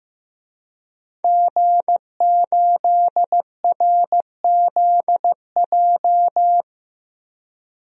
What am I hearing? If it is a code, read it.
Morse code, heard as a keyed tone, about 15 wpm, 700 Hz. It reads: G8RZJ
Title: DXCC-100 Call Signs - 5 Characters - 15wpm